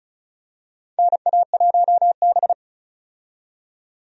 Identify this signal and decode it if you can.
Morse code, heard as a keyed tone, about 34 wpm, 700 Hz. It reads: NA1B
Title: USA Call Signs - 4 Characters - 35wpm